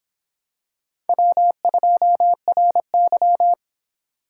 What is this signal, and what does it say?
Morse code, heard as a keyed tone, about 26 wpm, 700 Hz. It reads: W2RY